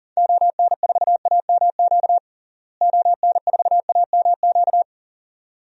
Morse code, heard as a keyed tone, about 40 wpm, 700 Hz. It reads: ON4AMQ ON4AMQ
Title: DXCC-100 Call Signs - 6 Characters - Repeated 2x - 40wpm